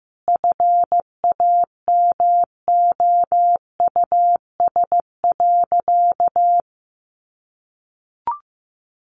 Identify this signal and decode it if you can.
Morse code, heard as a keyed tone, about 15 wpm, 700 Hz. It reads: FAMOUS.
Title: Sentences from Top 1500 Words 15wpm